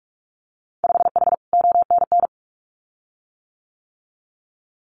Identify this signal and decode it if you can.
Morse code, heard as a keyed tone, about 46 wpm, 700 Hz. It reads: 5H ONN
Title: Winter Field Day Exchanges Encore 45wpm